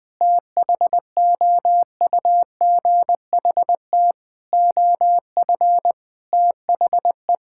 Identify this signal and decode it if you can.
Morse code, heard as a keyed tone, about 21 wpm, 700 Hz. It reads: THOUGHT OF THE
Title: Sentences from Top 1600 Words 20wpm